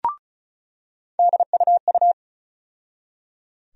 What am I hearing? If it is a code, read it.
Morse code, heard as a keyed tone, about 34 wpm, 700 Hz. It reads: DUU
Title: USA Call Sign Postfixes - 3 Letters 35wpm